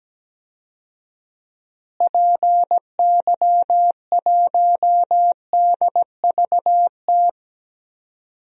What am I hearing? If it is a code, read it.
Morse code, heard as a keyed tone, about 17 wpm, 700 Hz. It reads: PY1DVT
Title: DXCC-100 Call Signs - 6 Characters - Repeated 3x - 17wpm